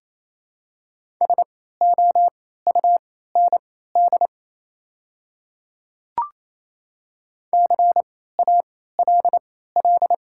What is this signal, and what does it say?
Morse code, heard as a keyed tone, about 28 wpm, spaced as for 16 wpm, 700 Hz. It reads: SOUND CALL
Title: 3x Character Spacing - Top 300 Words - 28wpm